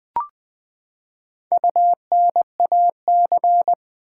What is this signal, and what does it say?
Morse code, heard as a keyed tone, about 20 wpm, 700 Hz. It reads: UNAC
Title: Build A Word - 4 Letters At A Time - 12-Letter Words - 20wpm